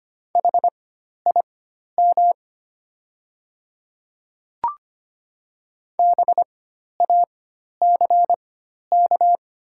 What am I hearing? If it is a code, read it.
Morse code, heard as a keyed tone, about 25 wpm, spaced as for 12 wpm, 700 Hz. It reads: HIM BACK
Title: 4x Character Spacing - Top 300 Words - 25wpm